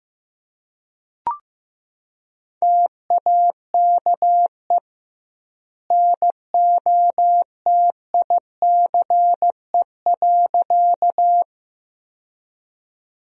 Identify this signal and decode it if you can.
Morse code, heard as a keyed tone, about 15 wpm, 700 Hz. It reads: TAKE NOTICE.
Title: 2x Word-Spacing - Sentences from Top 400 Words - 15wpm